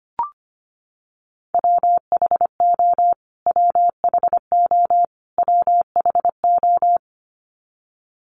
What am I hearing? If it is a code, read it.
Morse code, heard as a keyed tone, about 25 wpm, 700 Hz. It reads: WHO WHO WHO
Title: Top 500 Words Repeated 25wpm